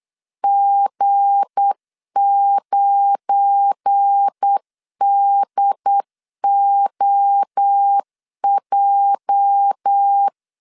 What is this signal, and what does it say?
Morse code, heard as a keyed tone, about 8 wpm, 800 Hz. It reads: G9DOJ
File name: LEZIONI / 8 YJ45 / 8 YJ 8wpm 3313